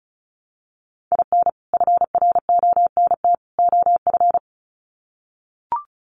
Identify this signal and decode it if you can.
Morse code, heard as a keyed tone, about 34 wpm, 700 Hz. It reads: IN FRONT OF
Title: Sets of 3 Words 35wpm